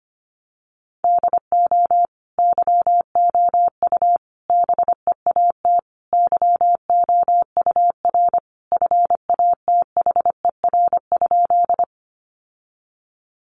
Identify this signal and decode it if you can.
Morse code, heard as a keyed tone, about 25 wpm, 700 Hz. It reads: DO YOU BEAT YOUR FATHER?
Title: Sentences from Top 1300 Words 25wpm